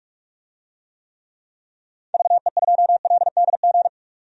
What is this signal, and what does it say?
Morse code, heard as a keyed tone, about 47 wpm, 700 Hz. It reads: VE2LDG
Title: DXCC-100 Call Signs - 6 Characters - 45wpm